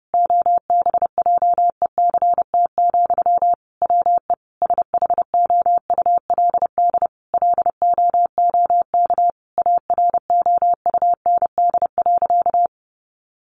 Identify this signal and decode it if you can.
Morse code, heard as a keyed tone, about 30 wpm, 700 Hz. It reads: OBJECT, WE SHOULD LOOK AROUND.
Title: Sentences from Top 800 Words 30wpm